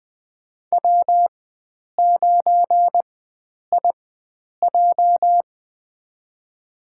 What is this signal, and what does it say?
Morse code, heard as a keyed tone, about 20 wpm, spaced as for 9 wpm, 700 Hz. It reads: W9IJ
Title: USA Call Signs - 4 Characters - 4x Character Spacing - 20wpm